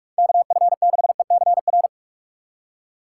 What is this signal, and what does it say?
Morse code, heard as a keyed tone, about 46 wpm, 690 Hz. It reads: KF6ECR
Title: USA Call Signs - 3x Character Spacing - 45wpm